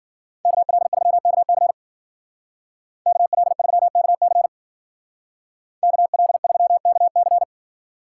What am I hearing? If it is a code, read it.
Morse code, heard as a keyed tone, about 50 wpm, 700 Hz. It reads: KL3KC KL3KC KL3KC
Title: Speed Racing - USA 2x2 Call Signs 40wpm